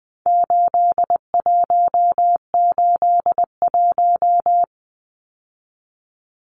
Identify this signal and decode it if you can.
Morse code, heard as a keyed tone, about 20 wpm, 700 Hz. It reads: 8181
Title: Numbers - 4-Digit 20wpm